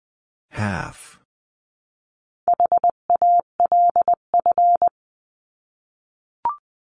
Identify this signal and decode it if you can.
Morse code, heard as a keyed tone, about 20 wpm, 700 Hz. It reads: HALF
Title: 259 Lesson 42d - Words 18-20wpm